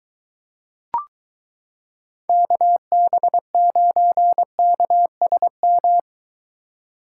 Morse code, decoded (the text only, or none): KB9KSM